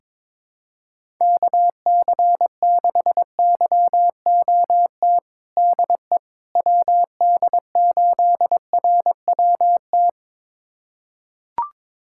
Morse code, heard as a keyed tone, about 22 wpm, 700 Hz. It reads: KC6YOT DE WD8RWT
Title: USA Callsign DE Callsign 22wpm